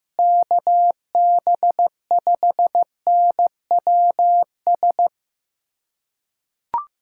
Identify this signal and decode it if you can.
Morse code, heard as a keyed tone, about 16 wpm, 700 Hz. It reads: KB5NWS
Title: USA Callsign DE Callsign 15wpm